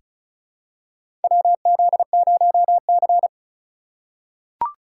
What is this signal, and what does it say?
Morse code, heard as a keyed tone, about 36 wpm, 700 Hz. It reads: WZ0C
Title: USA Call Signs - 2x Character Spacing - 35wpm